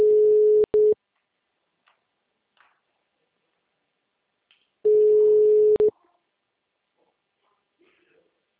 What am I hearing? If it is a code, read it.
Ringback tone (Europe, 425 Hz), with speech.